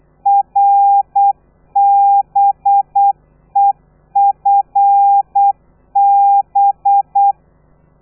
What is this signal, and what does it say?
Morse code, heard as a keyed tone, about 8 wpm, 800 Hz. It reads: RBEFB
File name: telegrafia / i2tfj / Lez13 35